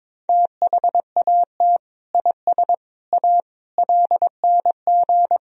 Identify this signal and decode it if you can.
Morse code, heard as a keyed tone, about 22 wpm, 700 Hz. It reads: THAT IS A LNG
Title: Sentences from Top 100 Words and Abbreviations 22wpm